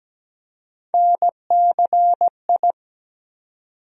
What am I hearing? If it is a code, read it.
Morse code, heard as a keyed tone, about 17 wpm, 700 Hz. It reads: NCI
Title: USA Call Sign Postfixes - 3 Letters 17wpm